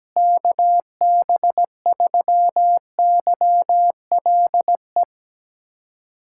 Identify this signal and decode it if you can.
Morse code, heard as a keyed tone, about 17 wpm, 700 Hz. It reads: KB3YLE
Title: USA Callsign DE Callsign - 6 Characters - 17wpm